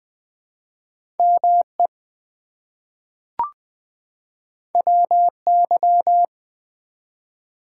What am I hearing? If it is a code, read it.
Morse code, heard as a keyed tone, about 20 wpm, 700 Hz. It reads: ME WY
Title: ARRL Sections v2 - 20wpm